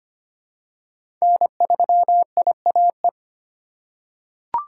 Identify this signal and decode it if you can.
Morse code, heard as a keyed tone, about 25 wpm, 700 Hz. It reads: N3IAE
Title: USA Call Signs - 2x Character Spacing - 25wpm